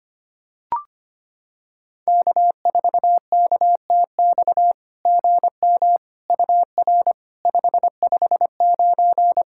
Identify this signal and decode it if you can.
Morse code, heard as a keyed tone, about 25 wpm, 690 Hz. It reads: K4KTX GM UR 559
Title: SOTA QSO Practice 25wpm